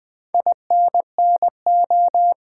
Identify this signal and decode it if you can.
Morse code, heard as a keyed tone, about 20 wpm, 700 Hz. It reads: INNO
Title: Build A Word - 4 Letters At A Time - 10-Letter Words - 20wpm